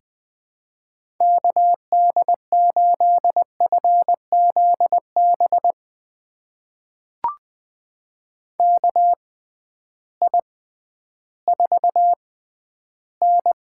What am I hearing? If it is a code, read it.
Morse code, heard as a keyed tone, about 20 wpm, spaced as for 12 wpm, 700 Hz. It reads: KD8FZB KI4N
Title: USA Call Signs - 6 Characters - 6x Character Spacing - 20wpm